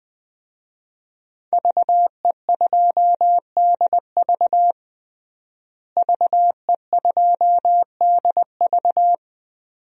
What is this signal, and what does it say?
Morse code, heard as a keyed tone, about 21 wpm, 700 Hz. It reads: VE2DV VE2DV
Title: DXCC-100 Call Signs - 5 Characters - Repeated 2x - 20wpm